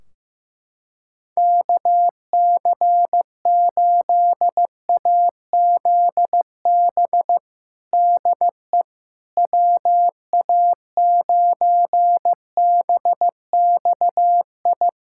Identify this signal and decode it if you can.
Morse code, heard as a keyed tone, about 16 wpm, 700 Hz. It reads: KC8AZB DE WA9BXI
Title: K1USN SST QSO Practice 15wpm